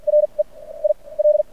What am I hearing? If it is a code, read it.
Morse code, heard as a keyed tone, about 50 wpm, spaced as for 24 wpm, 600 Hz. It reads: REIL
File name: archived / CRSnet